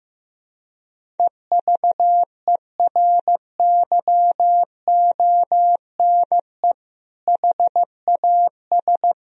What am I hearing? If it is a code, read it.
Morse code, heard as a keyed tone, about 15 wpm, 700 Hz. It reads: EVERYONE HAS
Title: Sentences from Top 1000 Words - Review Part 4 - 15wpm